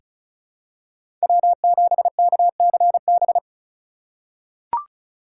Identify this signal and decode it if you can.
Morse code, heard as a keyed tone, about 34 wpm, 700 Hz. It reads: W7KCB
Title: USA Call Signs - 5 Characters - 3x Character Spacing - 35wpm